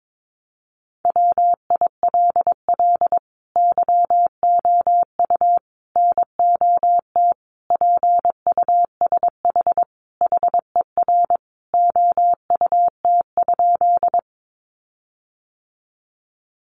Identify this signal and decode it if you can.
Morse code, heard as a keyed tone, about 22 wpm, 700 Hz. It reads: WILL YOU NOT PUSH HER OUT?
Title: Sentences from Top 1300 Words 22wpm